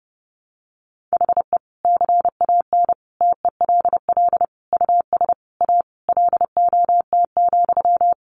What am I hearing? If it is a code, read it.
Morse code, heard as a keyed tone, about 30 wpm, 700 Hz. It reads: HE CAN TELL US A LOT,
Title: Sentences from Top 1300 Words 30wpm